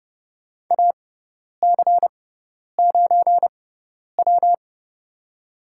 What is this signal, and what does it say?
Morse code, heard as a keyed tone, about 30 wpm, spaced as for 10 wpm, 700 Hz. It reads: AC9W
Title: USA Call Signs - 6x Character Spacing - 30wpm